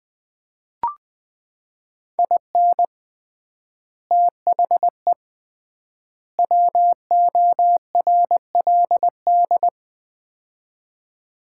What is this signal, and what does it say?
Morse code, heard as a keyed tone, about 20 wpm, 700 Hz. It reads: IN THE WORLD
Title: 3x Word-Spacing - Sets of 3 Words - 20wpm